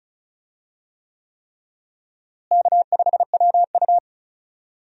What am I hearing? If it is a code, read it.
Morse code, heard as a keyed tone, about 36 wpm, 700 Hz. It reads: K5WU
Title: USA Call Signs - 4 Characters - Repeated 3x - 35wpm